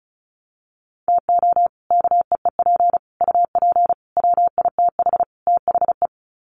Morse code, heard as a keyed tone, about 34 wpm, 700 Hz. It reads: TO KEEP UP WITH THE
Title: Sets of 5 Words Encore 35wpm